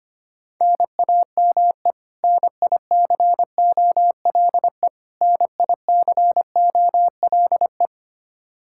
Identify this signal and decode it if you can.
Morse code, heard as a keyed tone, about 25 wpm, 700 Hz. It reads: NAME NICOLE NICOLE
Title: Practice QSO - Exchange 1 - Beginning - 25wpm